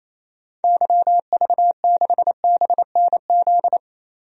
Morse code, heard as a keyed tone, about 28 wpm, 700 Hz. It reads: YV6BNZ
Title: DXCC-100 Call Signs - 6 Characters - 28wpm